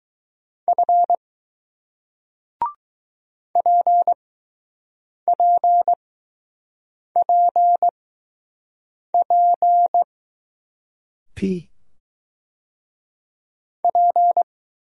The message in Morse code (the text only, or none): F PPPP P